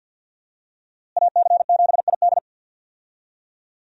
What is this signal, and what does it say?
Morse code, heard as a keyed tone, about 50 wpm, 700 Hz. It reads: AC7ID
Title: USA Call Signs - 5 Characters - Repeated 2x - 50wpm